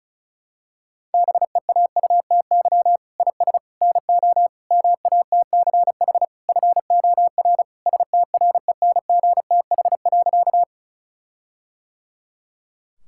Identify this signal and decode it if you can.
Morse code, heard as a keyed tone, about 36 wpm, 700 Hz. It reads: BEAUTY IS NO MATCH FOR STRENGTH.
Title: Sentences from Top 1600 Words 35wpm